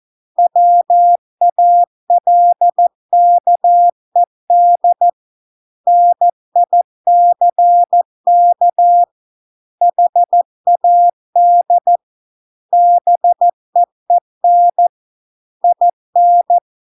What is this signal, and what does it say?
Morse code, heard as a keyed tone, about 14 wpm, 700 Hz. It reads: WALKED NICK HAD BEEN IN